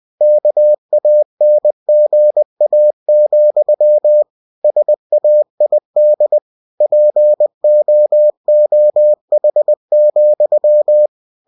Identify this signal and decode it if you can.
Morse code, heard as a keyed tone, about 20 wpm, 600 Hz. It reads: KANGA, SAID POOH,
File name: cw-books / winnie-the-pooh / winnie-20wpm-page-0066